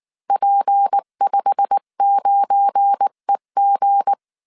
Morse code, heard as a keyed tone, about 19 wpm, 800 Hz. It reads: P59EG
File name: CW / DET / 1 9 1500C S C 18wpm 2539